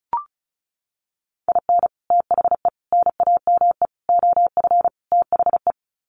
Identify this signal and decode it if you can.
Morse code, heard as a keyed tone, about 34 wpm, 700 Hz. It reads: IN THE NAME OF THE
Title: Sets of 5 Words Encore 35wpm